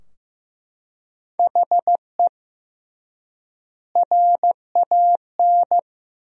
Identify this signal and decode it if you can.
Morse code, heard as a keyed tone, about 15 wpm, 700 Hz. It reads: HE RAN
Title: 3x Word-Spacing - Sentences from Top 500 Words - 15wpm